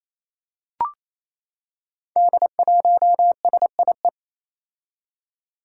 Morse code, heard as a keyed tone, about 28 wpm, 700 Hz. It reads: D1SIE
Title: Five Challenging Letter-Number Combinations 28wpm